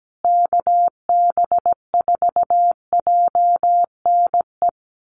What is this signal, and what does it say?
Morse code, heard as a keyed tone, about 17 wpm, 700 Hz. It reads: KB4JNE
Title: USA Call Signs Repeated 3x 17wpm